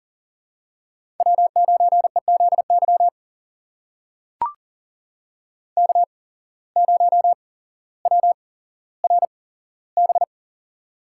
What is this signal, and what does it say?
Morse code, heard as a keyed tone, about 40 wpm, spaced as for 17 wpm, 700 Hz. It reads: W9EZY K0WRB
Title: USA Call Signs - 5 Characters - 8x Character Spacing - 40wpm